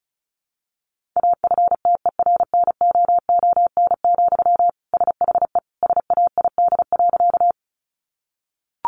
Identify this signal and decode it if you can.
Morse code, heard as a keyed tone, about 34 wpm, 700 Hz. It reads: AFTERNOON, SHE SAID.
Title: Sentences from Top 1500 Words 35wpm